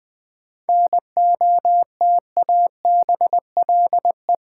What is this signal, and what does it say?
Morse code, heard as a keyed tone, about 20 wpm, 700 Hz. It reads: NOTABLE
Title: 7 Letter Words - First 3 Letters Given - 20wpm